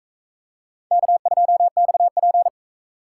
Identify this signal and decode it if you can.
Morse code, heard as a keyed tone, about 38 wpm, 700 Hz. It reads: K2XP